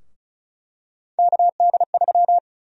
Morse code, heard as a keyed tone, about 36 wpm, 700 Hz. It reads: KD3
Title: USA Call Sign Prefixes - 35wpm